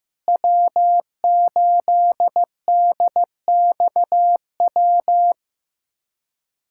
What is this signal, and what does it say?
Morse code, heard as a keyed tone, about 15 wpm, 700 Hz. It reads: W8DXW
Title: USA Callsign DE Callsign - 5 Characters - 15wpm